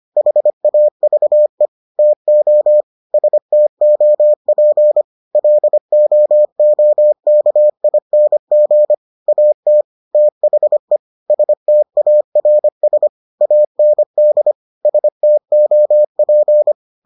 Morse code, heard as a keyed tone, about 25 wpm, 600 Hz. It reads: HAVE TO STOP LOOKING AT THE STARS AND STOP